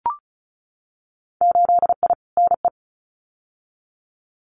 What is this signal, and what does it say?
Morse code, heard as a keyed tone, about 34 wpm, 700 Hz. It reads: 8I NE
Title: Winter Field Day Exchanges Encore 35wpm